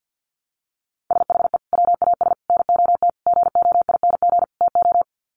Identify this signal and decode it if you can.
Morse code, heard as a keyed tone, about 50 wpm, 700 Hz. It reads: SHE WAS NOT GOING TO
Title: Sets of 5 Words Encore2 50wpm